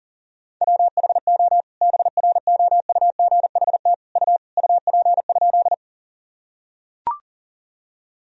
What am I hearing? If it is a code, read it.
Morse code, heard as a keyed tone, about 40 wpm, 700 Hz. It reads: WHO BROUGHT U UP?
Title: Sentences from Top 500 Words and Abbreviations 40wpm